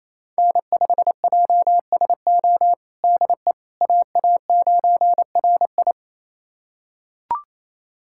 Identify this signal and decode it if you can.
Morse code, heard as a keyed tone, about 27 wpm, 700 Hz. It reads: N5JSO DE AA9RI
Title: USA Callsign DE Callsign - 5 Characters - 28wpm